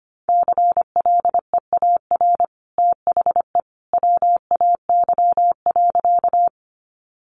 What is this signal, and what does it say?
Morse code, heard as a keyed tone, about 25 wpm, 700 Hz. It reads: CLEAR THE WAY.
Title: Sentences from Top 500 Words 25wpm